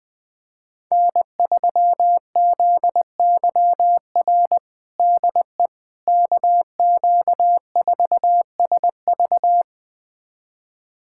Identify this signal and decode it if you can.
Morse code, heard as a keyed tone, about 19 wpm, 700 Hz. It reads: N3ZYR DE KQ4SV
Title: USA Callsign DE Callsign - 5 Characters - 20wpm